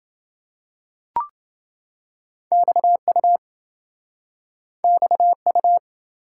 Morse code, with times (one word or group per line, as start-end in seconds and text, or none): 2.5-3.4 XU
4.8-5.8 XU